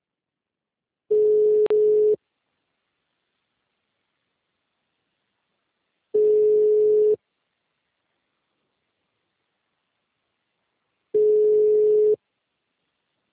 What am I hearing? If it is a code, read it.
Ringback tone (Europe, 425 Hz).